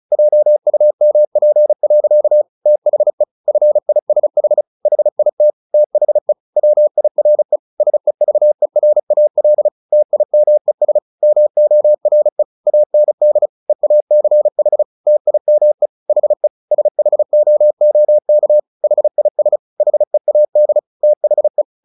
Morse code, heard as a keyed tone, about 34 wpm, 600 Hz. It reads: JUMP. THE FISH HIT THE WIRE SEVERAL TIMES MORE AND EACH TIME HE SHOOK HIS HEAD THE
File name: the old man and the sea 2 35wpm